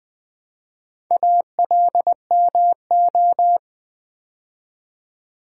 Morse code, heard as a keyed tone, about 20 wpm, 700 Hz. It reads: ALMO
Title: Build A Word - 2 Letters At A Time - 6-Letter Words - 20wpm